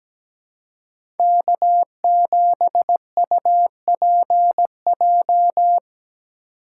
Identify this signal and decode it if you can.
Morse code, heard as a keyed tone, about 17 wpm, 700 Hz. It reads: K7UPJ
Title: K1USN SST QSO Practice - 17wpm